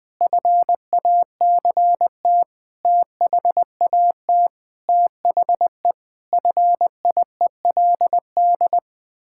Morse code, heard as a keyed tone, about 21 wpm, 700 Hz. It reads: FACT THAT THE FIELD
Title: Sentences with 6-Word Phrases - 20wpm